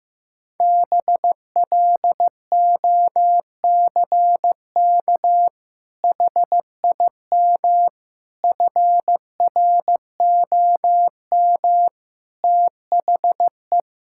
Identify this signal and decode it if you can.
Morse code, heard as a keyed tone, about 15 wpm, 700 Hz. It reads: BLOCK HIM FROM THE
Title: Sentences from Top 1600 Words 15wpm